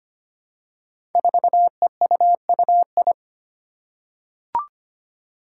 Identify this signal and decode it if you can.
Morse code, heard as a keyed tone, about 25 wpm, 700 Hz. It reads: 4EUUI
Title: Five Challenging Letter-Number Combinations 25wpm